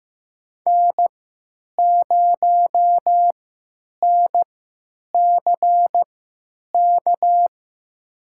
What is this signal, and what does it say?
Morse code, heard as a keyed tone, about 16 wpm, spaced as for 9 wpm, 700 Hz. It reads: N0NCK
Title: USA Call Signs - 5 Characters - 3x Character Spacing - 15wpm